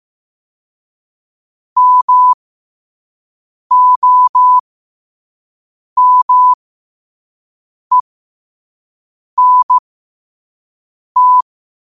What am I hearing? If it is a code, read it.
Morse code, heard as a keyed tone, about 15 wpm, spaced as for 5 wpm, 1000 Hz. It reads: MOMENT
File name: qotdfiles / 5wpm